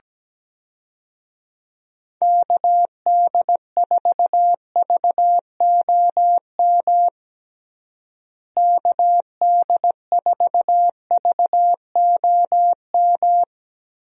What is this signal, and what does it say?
Morse code, heard as a keyed tone, about 17 wpm, 700 Hz. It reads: KD4VOM KD4VOM
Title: USA Call Signs - 6 Characters - Repeated 3x - 17wpm